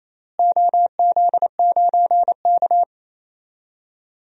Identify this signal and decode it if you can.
Morse code, heard as a keyed tone, about 28 wpm, 700 Hz. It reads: OZ9K
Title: DXCC-100 Call Signs - 4 Characters - Repeated 3x - 28wpm